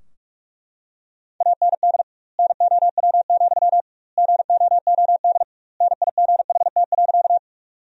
Morse code, heard as a keyed tone, about 42 wpm, 700 Hz. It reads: AND NOW, GOOD NIGHT.
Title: 2x Word-Spacing - Sentences from Top 300 Words - 45wpm